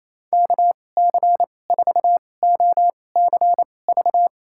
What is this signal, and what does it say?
Morse code, heard as a keyed tone, about 28 wpm, spaced as for 20 wpm, 700 Hz. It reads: KC4OCV
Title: USA Call Signs - 6 Characters - 2x Character Spacing - 28wpm